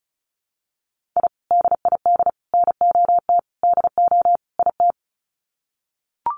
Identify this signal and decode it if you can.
Morse code, heard as a keyed tone, about 34 wpm, 700 Hz. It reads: I DID NOT DO IT
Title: Sets of 5 Words Encore2 35wpm